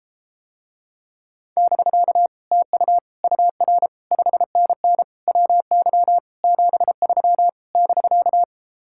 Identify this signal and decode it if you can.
Morse code, heard as a keyed tone, about 33 wpm, 700 Hz. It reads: <BK> TU UR 5NN WY 73 <BK>